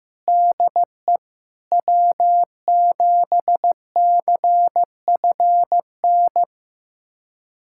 Morse code, heard as a keyed tone, about 15 wpm, 700 Hz. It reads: DE W7CFN
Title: USA Callsign DE Callsign - 5 Characters - 15wpm